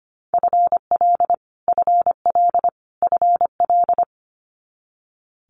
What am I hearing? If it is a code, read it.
Morse code, heard as a keyed tone, about 25 wpm, 700 Hz. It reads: FL FL FL
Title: Multi-letter Phonemes Repeated 3x 25wpm